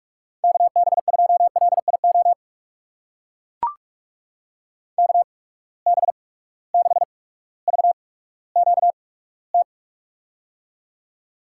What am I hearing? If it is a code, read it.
Morse code, heard as a keyed tone, about 44 wpm, spaced as for 19 wpm, 700 Hz. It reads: KB2LIO KB6VQT
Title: USA Call Signs - 6 Characters - 8x Character Spacing - 45wpm